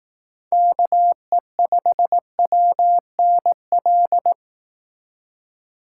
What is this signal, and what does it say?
Morse code, heard as a keyed tone, about 18 wpm, 700 Hz. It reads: KE5WNL
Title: Speed Racing - USA Call Signs 15wpm